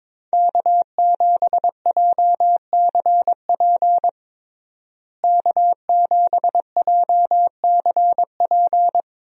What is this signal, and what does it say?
Morse code, heard as a keyed tone, about 22 wpm, 700 Hz. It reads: K7JCP K7JCP
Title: USA Call Signs - 5 Characters - Repeated 3x - 22wpm